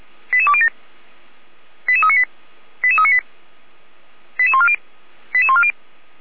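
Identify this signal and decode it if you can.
Five-tone selective call (ZVEI), heard as quick runs of five tones, tones of about 70 ms. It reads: 88207 88398 88398 88140 88140
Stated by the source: ZVEI five-tone call received by SDR, logged on server of feuerwehr-orlamuende.org (Feuerwehr Orlamünde).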